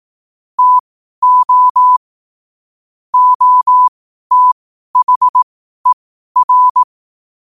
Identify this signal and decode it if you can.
Morse code, heard as a keyed tone, about 18 wpm, spaced as for 12 wpm, 1000 Hz. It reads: TO OTHER
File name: qotdfiles / 12wpm